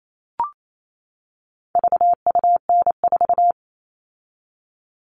Morse code, heard as a keyed tone, about 28 wpm, 700 Hz. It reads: VUN4
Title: All Cut Numbers - 4-Digit 28wpm